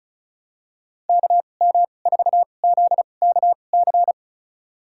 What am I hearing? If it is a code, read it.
Morse code, heard as a keyed tone, about 36 wpm, spaced as for 25 wpm, 700 Hz. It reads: KM4ZKC
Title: USA Call Signs - 6 Characters - 2x Character Spacing - 35wpm